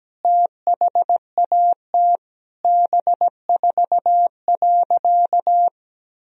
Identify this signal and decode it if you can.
Morse code, heard as a keyed tone, about 17 wpm, 700 Hz. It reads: THAT B4.
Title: Sentences from Top 400 Words and Abbreviations 17wpm